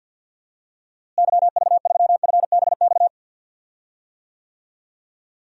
Morse code, heard as a keyed tone, about 52 wpm, 700 Hz. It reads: YV3FBX
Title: DXCC-100 Call Signs - 6 Characters - Repeated 3x - 50wpm